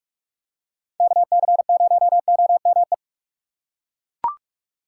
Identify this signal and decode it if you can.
Morse code, heard as a keyed tone, about 42 wpm, 700 Hz. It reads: KC0OME
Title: USA Call Signs - 6 Characters - 6x Character Spacing - 45wpm